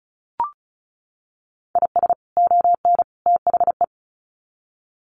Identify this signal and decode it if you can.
Morse code, heard as a keyed tone, about 34 wpm, 700 Hz. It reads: IS ON THE
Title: Sets of 3 Words 35wpm